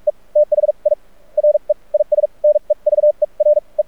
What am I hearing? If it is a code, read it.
Morse code, heard as a keyed tone, about 44 wpm, spaced as for 32 wpm, 600 Hz. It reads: E THIRE IS NEVER E
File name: archived / CRSnet